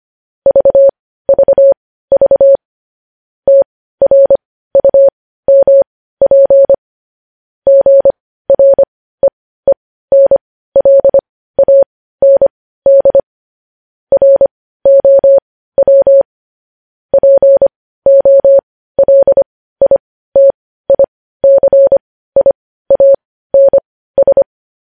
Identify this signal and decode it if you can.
Morse code, heard as a keyed tone, about 25 wpm, spaced as for 15 wpm, 550 Hz. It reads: VVV TRUMP GREENLAND ROW POLITICIANS
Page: news headlines in morse code at 15 wpm